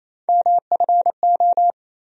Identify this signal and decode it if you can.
Morse code, heard as a keyed tone, about 28 wpm, 700 Hz. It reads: MFO
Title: USA Call Sign Postfixes - 3 Letters 28wpm